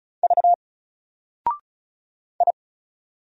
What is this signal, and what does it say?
Morse code, heard as a keyed tone, about 36 wpm, 700 Hz. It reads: V I
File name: Single-Letter-Number-35wpm